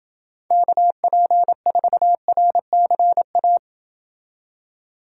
Morse code, heard as a keyed tone, about 29 wpm, 700 Hz. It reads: KP4RCA